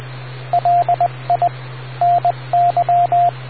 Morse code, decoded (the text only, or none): LI NY